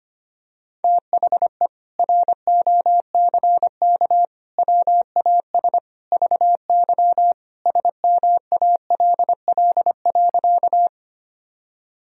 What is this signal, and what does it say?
Morse code, heard as a keyed tone, about 25 wpm, 700 Hz. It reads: THE ROCK WAS VY SMALL.
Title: Sentences from Top 400 Words and Abbreviations 25wpm